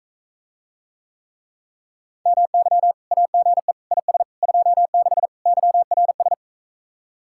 Morse code, heard as a keyed tone, about 44 wpm, 700 Hz. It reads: MY AGE IS 26 YRS